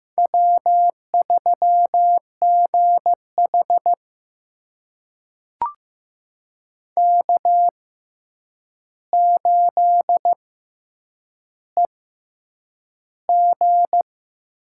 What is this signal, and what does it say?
Morse code, heard as a keyed tone, about 15 wpm, spaced as for 8 wpm, 700 Hz. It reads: W3GH K8EG
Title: USA Call Signs - 4 Characters - 6x Character Spacing - 15wpm